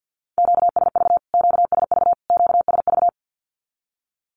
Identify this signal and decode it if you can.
Morse code, heard as a keyed tone, about 48 wpm, 700 Hz. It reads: QSV QSV QSV
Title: Q-Codes Repeated 3x 50wpm